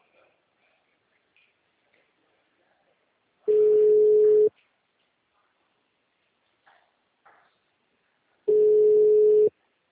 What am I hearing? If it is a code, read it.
Ringback tone (Europe, 425 Hz).